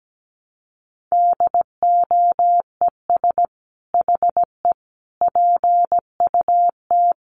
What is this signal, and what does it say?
Morse code, heard as a keyed tone, about 17 wpm, 700 Hz. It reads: DOES HE PUT
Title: Sentences from Top 1300 Words 17wpm